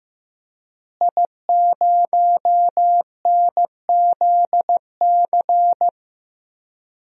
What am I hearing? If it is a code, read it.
Morse code, heard as a keyed tone, about 15 wpm, 700 Hz. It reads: I0NZC
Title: DXCC-100 Call Signs - 5 Characters - Repeated 3x - 15wpm